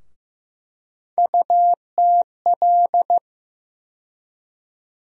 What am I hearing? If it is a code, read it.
Morse code, heard as a keyed tone, about 15 wpm, 700 Hz. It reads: UTL
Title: USA Call Sign Postfixes - 3 Letters 15wpm